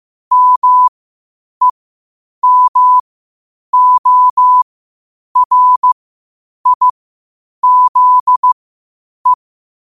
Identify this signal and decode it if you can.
Morse code, heard as a keyed tone, about 15 wpm, spaced as for 8 wpm, 1000 Hz. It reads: MEMORIZE
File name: qotdfiles / 8wpm